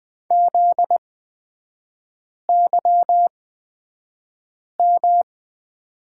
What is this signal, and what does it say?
Morse code, heard as a keyed tone, about 20 wpm, spaced as for 5 wpm, 700 Hz. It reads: ZYM
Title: 250 Lesson 40e - BK - Three Characters 20wpm